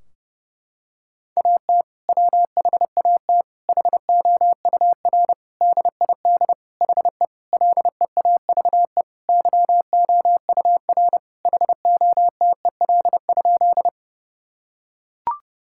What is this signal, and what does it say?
Morse code, heard as a keyed tone, about 30 wpm, 700 Hz. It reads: AT WHAT HOUR DID HE LEAVE YOUR HOTEL?
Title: Sentences from Top 1600 Words 30wpm